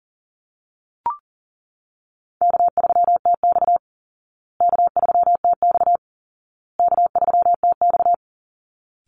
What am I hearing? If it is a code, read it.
Morse code, heard as a keyed tone, about 40 wpm, 700 Hz. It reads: K3TX K3TX K3TX
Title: Easy Call Signs Repeated 3x - 40wpm